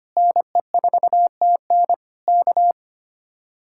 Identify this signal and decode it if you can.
Morse code, heard as a keyed tone, about 24 wpm, 700 Hz. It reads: NE4TN K
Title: Speed-Racing Common QSO Phrases - 25wpm